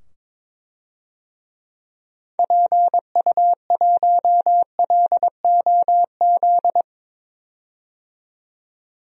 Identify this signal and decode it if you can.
Morse code, heard as a keyed tone, about 22 wpm, 700 Hz. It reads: PU1LOZ
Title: DXCC-100 Call Signs - 6 Characters - 22wpm